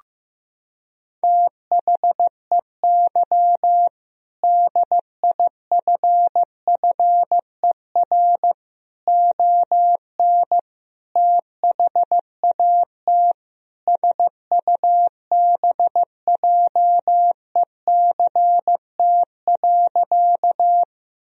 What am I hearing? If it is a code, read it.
Morse code, heard as a keyed tone, about 15 wpm, 700 Hz. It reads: THEY DIFFER ON THAT SUBJECT.
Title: Sentences from Top 1000 Words - Review Part 1 - 15wpm